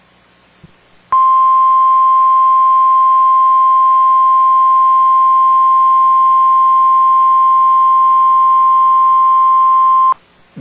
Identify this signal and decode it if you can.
Emergency Alert System (SAME) tones (the 1050 Hz attention tone). It is heard from 1.1 to 10.2 s.